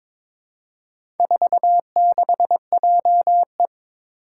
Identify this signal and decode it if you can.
Morse code, heard as a keyed tone, about 22 wpm, 700 Hz. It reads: 46JE
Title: Four Challenging Letter-Number Combinations 22wpm